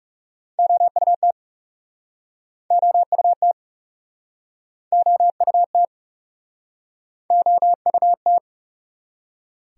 Morse code, heard as a keyed tone, about 42 wpm, 700 Hz. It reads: OUT OUT OUT OUT